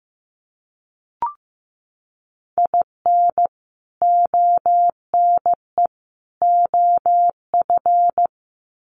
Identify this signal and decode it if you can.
Morse code, heard as a keyed tone, about 15 wpm, 700 Hz. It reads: IN ONE OF
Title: Sets of 3 Words 15wpm